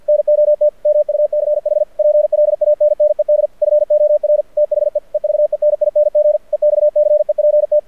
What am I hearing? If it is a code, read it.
Morse code, heard as a keyed tone, about 53 wpm, 600 Hz. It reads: NOT MUCH PLANNED FOR THE EVENING EXCEPT
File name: archived / CRSnet